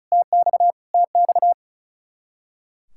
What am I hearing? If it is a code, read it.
Morse code, heard as a keyed tone, about 32 wpm, 700 Hz. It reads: TX TX
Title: Speed-Racing Common QSO Phrases - 35wpm